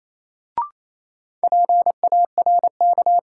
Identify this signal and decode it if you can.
Morse code, heard as a keyed tone, about 28 wpm, 700 Hz. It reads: PARK
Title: Sentences from Top 1300 Words - New Word Drill - 28wpm